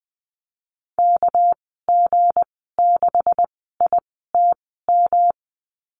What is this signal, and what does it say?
Morse code, heard as a keyed tone, about 20 wpm, spaced as for 14 wpm, 700 Hz. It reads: KG6ITM
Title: 219 Lesson 35g - Number 0 - Call Signs 20wpm